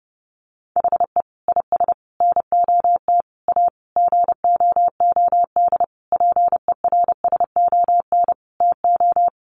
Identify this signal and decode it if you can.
Morse code, heard as a keyed tone, about 30 wpm, 700 Hz. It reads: HE IS NOT A GOOD PERSON TO
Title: Sentences from Top 1400 Words 30wpm